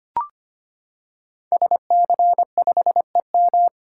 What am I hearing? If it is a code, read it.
Morse code, heard as a keyed tone, about 25 wpm, 700 Hz. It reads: SC5EM
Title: Code Groups - 5 Letter:Numbers - 25wpm